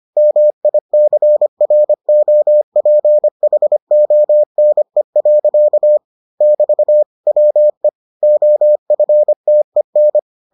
Morse code, heard as a keyed tone, about 25 wpm, 600 Hz. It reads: MICROPHONE. <BT> WE OFTEN